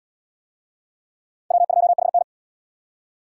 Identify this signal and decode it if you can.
Morse code, heard as a keyed tone, about 77 wpm, 700 Hz. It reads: W2HN